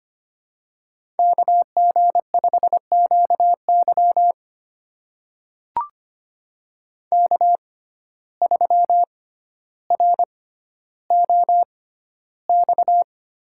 Morse code, heard as a keyed tone, about 25 wpm, spaced as for 13 wpm, 700 Hz. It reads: KG5QY K3ROX
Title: USA Call Signs - 5 Characters - 6x Character Spacing - 25wpm